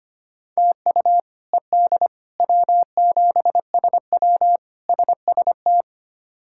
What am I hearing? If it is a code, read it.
Morse code, heard as a keyed tone, about 25 wpm, 700 Hz. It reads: TU ED W7SW SST
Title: K1USN SST QSO Practice - 25wpm